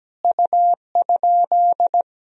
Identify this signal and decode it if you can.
Morse code, heard as a keyed tone, about 17 wpm, 700 Hz. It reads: U?
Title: Sentences from Top 500 Words and Abbreviations 17wpm